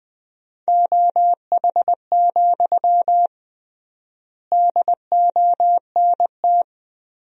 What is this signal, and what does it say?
Morse code, heard as a keyed tone, about 20 wpm, 700 Hz. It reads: OH, DONT
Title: 3x Word-Spacing - Sentences from Top 500 Words - 20wpm